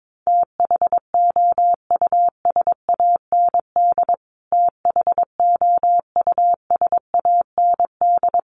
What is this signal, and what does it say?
Morse code, heard as a keyed tone, about 22 wpm, 700 Hz. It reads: THOUSAND THOUSAND
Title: Top 500 Words Repeated 22wpm